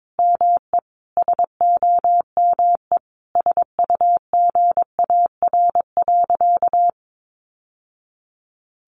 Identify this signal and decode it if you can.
Morse code, heard as a keyed tone, about 22 wpm, 700 Hz. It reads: ME SOME SUGAR.
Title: Sentences from Top 1200 Words 22wpm